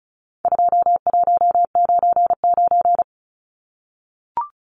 Morse code, heard as a keyed tone, about 34 wpm, 700 Hz. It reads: 2199
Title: Numbers - 4-Digit 35wpm